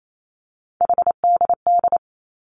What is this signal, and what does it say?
Morse code, heard as a keyed tone, about 28 wpm, 700 Hz. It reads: HDD
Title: Three Challenging Letter-Number Combinations 28wpm